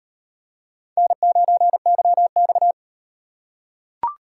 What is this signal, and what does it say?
Morse code, heard as a keyed tone, about 35 wpm, 700 Hz. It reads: N9YX